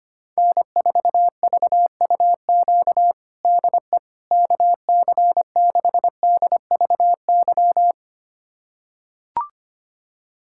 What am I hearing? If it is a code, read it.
Morse code, heard as a keyed tone, about 25 wpm, 700 Hz. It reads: N4VUQ DE KC6DVY
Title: USA Callsign DE Callsign 25wpm